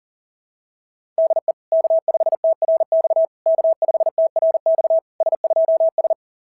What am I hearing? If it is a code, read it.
Morse code, heard as a keyed tone, about 34 wpm, 660 Hz. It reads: DE K5TRX K5TRX S2S
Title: SOTA QSO Practice 40wpm